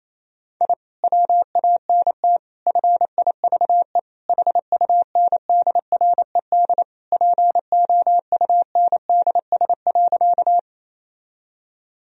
Morse code, heard as a keyed tone, about 28 wpm, 700 Hz. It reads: I WANT FIVE HUNDRED POUNDS.
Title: Sentences from Top 1000 Words - Review Part 2 - 28wpm